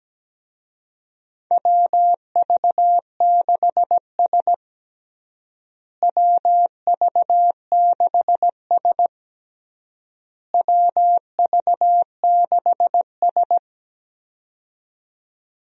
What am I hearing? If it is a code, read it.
Morse code, heard as a keyed tone, about 17 wpm, 700 Hz. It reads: WV6S WV6S WV6S
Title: USA Call Signs - 4 Characters - Repeated 3x - 17wpm